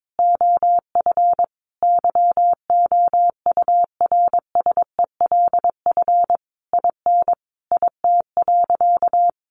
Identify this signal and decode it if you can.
Morse code, heard as a keyed tone, about 22 wpm, 700 Hz. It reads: OF YOURSELF IN IT.
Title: Sentences from Top 1300 Words 22wpm